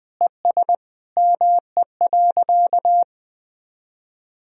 Morse code, heard as a keyed tone, about 20 wpm, 700 Hz. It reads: ES ME.
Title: Sentences from Top 500 Words and Abbreviations 20wpm